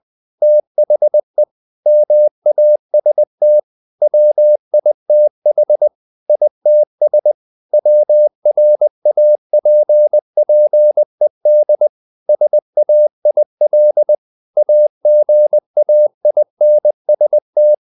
Morse code, heard as a keyed tone, about 20 wpm, 600 Hz. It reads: THE MAST WITH ITS WRAPPED SAIL AGAINST
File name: the old man and the sea / the old man and the sea part 1 20wpm